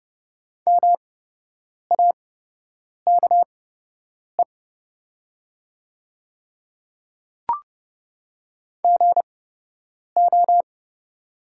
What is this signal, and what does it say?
Morse code, heard as a keyed tone, about 30 wpm, spaced as for 8 wpm, 700 Hz. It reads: MAKE GO
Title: 8x Character Spacing - Top 100 Words - 30wpm